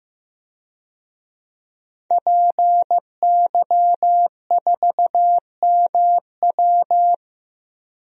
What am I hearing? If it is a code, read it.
Morse code, heard as a keyed tone, about 15 wpm, 700 Hz. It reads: PY4MW
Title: DXCC-100 Call Signs - 5 Characters - Repeated 3x - 15wpm